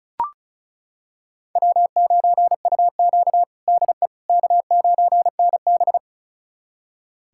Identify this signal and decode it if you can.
Morse code, heard as a keyed tone, about 36 wpm, 700 Hz. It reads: W9UQ DE K9NB
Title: USA Callsign DE Callsign - 4 Characters - 35wpm